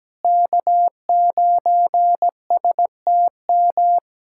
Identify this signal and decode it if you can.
Morse code, heard as a keyed tone, about 17 wpm, 700 Hz. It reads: K9STM
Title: K1USN SST QSO Practice - 17wpm